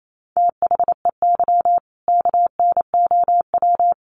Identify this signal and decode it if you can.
Morse code, heard as a keyed tone, about 28 wpm, 700 Hz. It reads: THEY KNOW
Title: Sentences from Top 500 Words 28wpm